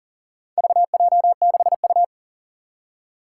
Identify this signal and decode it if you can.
Morse code, heard as a keyed tone, about 42 wpm, 700 Hz. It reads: VJ6U
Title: Four Challenging Letter-Number Combinations 40wpm